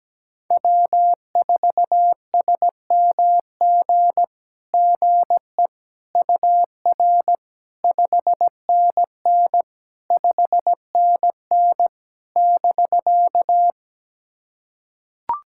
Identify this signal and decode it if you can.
Morse code, heard as a keyed tone, about 17 wpm, 700 Hz. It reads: W4SMG GE UR 5NN 5NN <BK>
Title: SOTA QSO Practice 17wpm